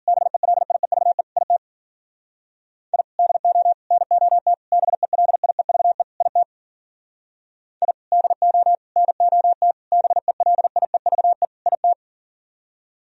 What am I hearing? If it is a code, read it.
Morse code, heard as a keyed tone, about 52 wpm, 700 Hz. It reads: BELIEVE IT I DO NOT BELIEVE IT I DO NOT BELIEVE IT